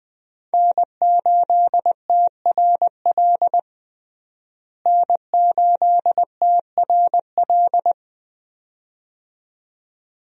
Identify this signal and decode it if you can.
Morse code, heard as a keyed tone, about 20 wpm, 700 Hz. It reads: N8TRL N8TRL
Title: USA Call Signs - 5 Characters - Repeated 2x - 20wpm